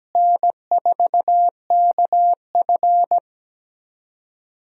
Morse code, heard as a keyed tone, about 17 wpm, 700 Hz. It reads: N4KF
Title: USA Call Signs - 4 Characters - Repeated 2x - 17wpm